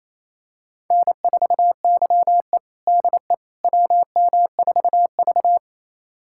Morse code, heard as a keyed tone, about 28 wpm, 700 Hz. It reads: N4YE DE WM4V
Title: USA Callsign DE Callsign - 4 Characters - 28wpm